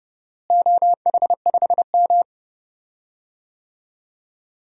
Morse code, heard as a keyed tone, about 30 wpm, 700 Hz. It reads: OH5M
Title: DXCC-100 Call Signs - 4 Characters - Repeated 3x - 30wpm